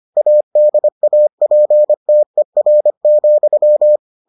Morse code, heard as a keyed tone, about 25 wpm, 600 Hz. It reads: ADAPTER,